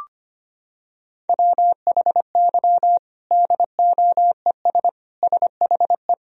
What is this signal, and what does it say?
Morse code, heard as a keyed tone, about 25 wpm, 700 Hz. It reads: WHY DOES SHE
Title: Sentences from Top 1000 Words - Review Part 1 - 25wpm